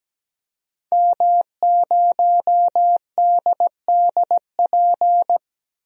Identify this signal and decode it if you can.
Morse code, heard as a keyed tone, about 17 wpm, 700 Hz. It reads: M0DDP
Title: DXCC-100 Call Signs - 5 Characters - 17wpm